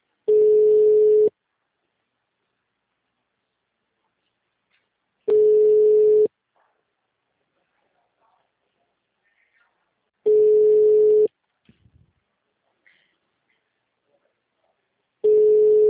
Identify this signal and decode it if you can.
Ringback tone (Europe, 425 Hz).
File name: out- -2019- - - .647